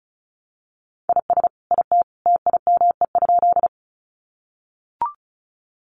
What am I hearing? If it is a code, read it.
Morse code, heard as a keyed tone, about 34 wpm, 700 Hz. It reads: IS IT TIME?
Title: Sentences from Top 100 Words 35wpm